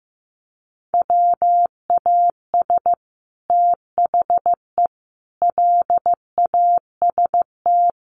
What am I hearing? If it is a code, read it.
Morse code, heard as a keyed tone, about 15 wpm, 700 Hz. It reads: WAS THE LAST
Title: Sets of 5 Words Encore2 15wpm